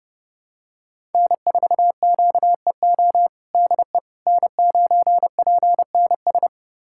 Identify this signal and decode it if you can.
Morse code, heard as a keyed tone, about 30 wpm, 700 Hz. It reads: N4QEO DE N9PNS
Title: USA Callsign DE Callsign - 5 Characters - 30wpm